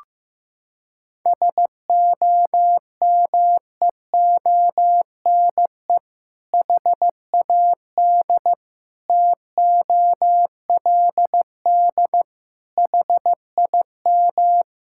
Morse code, heard as a keyed tone, about 15 wpm, 700 Hz. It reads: SOMEONE HAD TOLD HIM
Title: Sentences from Top 1000 Words - Review Part 6 - 15wpm